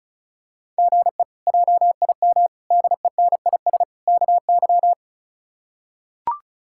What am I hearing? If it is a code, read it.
Morse code, heard as a keyed tone, about 36 wpm, 700 Hz. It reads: GE JIM DENIS KY
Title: K1USN SST QSO Practice - 35wpm